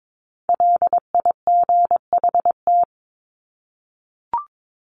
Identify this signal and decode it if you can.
Morse code, heard as a keyed tone, about 22 wpm, 700 Hz. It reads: LIGHT
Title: Top 300 Words 22wpm